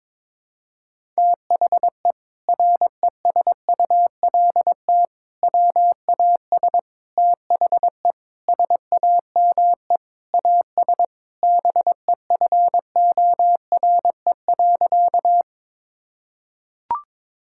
Morse code, heard as a keyed tone, about 22 wpm, 700 Hz. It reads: THE RESULT WAS THE SAME AS BEFORE.
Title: Sentences from Top 1000 Words - Review Part 5 - 22wpm